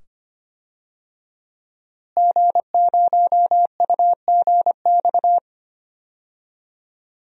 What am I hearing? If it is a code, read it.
Morse code, heard as a keyed tone, about 25 wpm, 700 Hz. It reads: G0UGX
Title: DXCC-100 Call Signs - 5 Characters - Repeated 2x - 25wpm